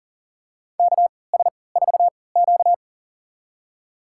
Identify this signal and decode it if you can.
Morse code, heard as a keyed tone, about 40 wpm, spaced as for 23 wpm, 700 Hz. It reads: KS4Q
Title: USA Call Signs - 4 Characters - 3x Character Spacing - 40wpm